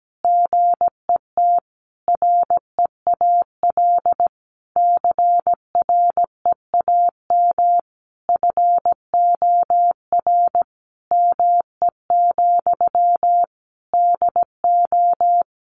Morse code, heard as a keyed tone, about 17 wpm, 700 Hz. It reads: GET REAL CREAM FOR ME, DO
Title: Sentences from Top 1500 Words 17wpm